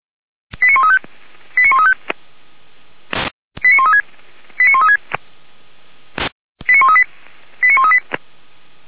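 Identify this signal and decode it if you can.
Five-tone selective call (ZVEI), heard as quick runs of five tones, tones of about 70 ms. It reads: 89135 89135 89136 89136 89138 89138
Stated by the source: ZVEI five-tone call received on SDR.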